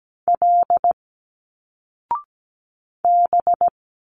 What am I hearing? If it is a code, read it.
Morse code, heard as a keyed tone, about 17 wpm, 700 Hz. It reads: L B
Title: Single Letters 17wpm